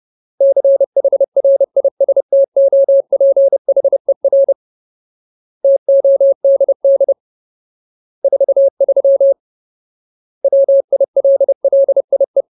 Morse code, heard as a keyed tone, about 30 wpm, 550 Hz. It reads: CHRISTOPHER TODD 43 WILLIE
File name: Ultimate Word List / 30WPM Ultimate